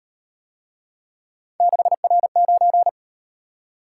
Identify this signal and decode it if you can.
Morse code, heard as a keyed tone, about 38 wpm, 700 Hz. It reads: 6R9